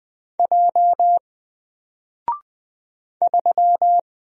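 Morse code, heard as a keyed tone, about 20 wpm, 700 Hz. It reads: J 3
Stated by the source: Single letter and number Morse drill at 20 wpm, from N4LCD.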